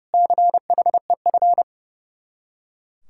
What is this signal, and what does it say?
Morse code, heard as a keyed tone, about 36 wpm, 700 Hz. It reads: CHEF
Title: Speed-Racing Common QSO Phrases - 30wpm